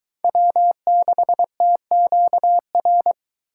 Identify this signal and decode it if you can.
Morse code, heard as a keyed tone, about 22 wpm, 700 Hz. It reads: W6TQR